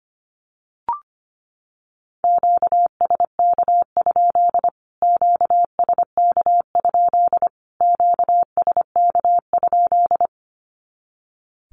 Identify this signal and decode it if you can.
Morse code, heard as a keyed tone, about 25 wpm, 700 Hz. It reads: QSK? QSK? QSK?
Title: Q-Codes Repeated 3x 25wpm